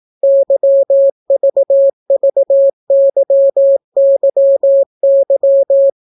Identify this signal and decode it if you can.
Morse code, heard as a keyed tone, about 18 wpm, 550 Hz. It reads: YVVYYY